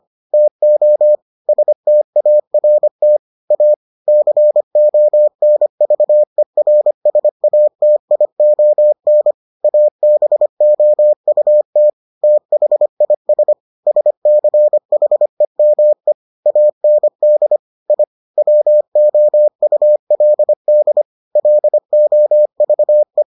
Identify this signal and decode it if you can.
Morse code, heard as a keyed tone, about 25 wpm, 600 Hz. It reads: TO START A CONVERSATION ABOUT THIS SCHEME AND I WOULD LOVE